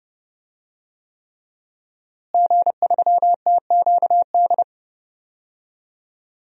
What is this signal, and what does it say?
Morse code, heard as a keyed tone, about 31 wpm, 700 Hz. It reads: G3TQD
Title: DXCC-100 Call Signs - 5 Characters - Repeated 3x - 30wpm